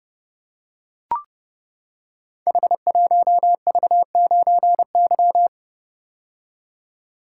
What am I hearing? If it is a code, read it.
Morse code, heard as a keyed tone, about 30 wpm, 700 Hz. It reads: H1V9Y
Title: Code Groups - 5 Letter:Numbers - 30wpm